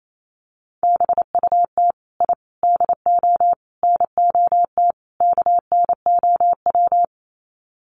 Morse code, heard as a keyed tone, about 28 wpm, 700 Hz. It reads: BUT I DO NOT KNOW
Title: Sets of 5 Words 28wpm